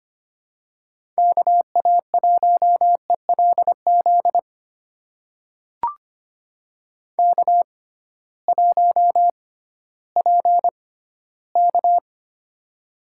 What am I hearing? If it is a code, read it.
Morse code, heard as a keyed tone, about 25 wpm, spaced as for 14 wpm, 700 Hz. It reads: KA1ELZ K1PK
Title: USA Call Signs - 6x Character Spacing - 25wpm